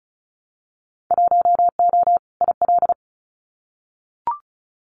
Morse code, heard as a keyed tone, about 34 wpm, 700 Hz. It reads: 1O IL
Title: Winter Field Day Exchanges 35wpm